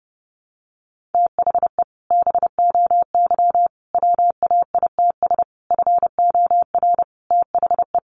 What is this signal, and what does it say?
Morse code, heard as a keyed tone, about 30 wpm, 700 Hz. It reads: THE BOY WAITS FOR THE
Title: Easy Phrases 30wpm